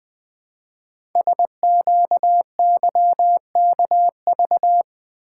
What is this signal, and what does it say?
Morse code, heard as a keyed tone, about 20 wpm, 700 Hz. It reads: SQYKV
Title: Code Groups - 5 Letters - 20wpm